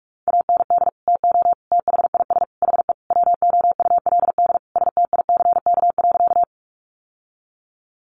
Morse code, heard as a keyed tone, about 44 wpm, 700 Hz. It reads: AND TO THIS HE WOULD STICK.
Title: Sentences from Top 1200 Words 45wpm